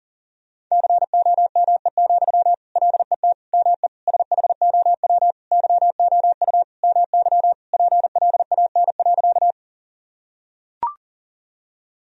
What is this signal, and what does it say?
Morse code, heard as a keyed tone, about 37 wpm, 700 Hz. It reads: COME, LET ME SHOW YOU MY PLAN.
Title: Sentences from Top 1000 Words - Review Part 2 - 40wpm